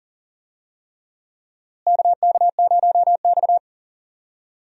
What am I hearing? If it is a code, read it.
Morse code, heard as a keyed tone, about 40 wpm, 700 Hz. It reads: KK0X
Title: USA Call Signs - 4 Characters - Repeated 3x - 40wpm